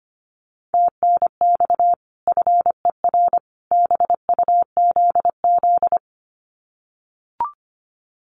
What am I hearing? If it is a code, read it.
Morse code, heard as a keyed tone, about 25 wpm, 700 Hz. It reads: TNX FER BUZZ
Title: Common QSO Phrases 25wpm